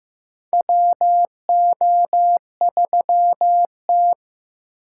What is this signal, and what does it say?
Morse code, heard as a keyed tone, about 15 wpm, 700 Hz. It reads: WO3T
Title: USA Call Signs - 4 Characters - 4x Character Spacing - 15wpm